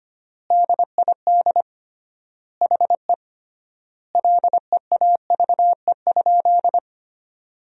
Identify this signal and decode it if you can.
Morse code, heard as a keyed tone, about 25 wpm, 700 Hz. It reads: DID HE LEAVE?
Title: 3x Word-Spacing - Sentences from Top 400 Words - 25wpm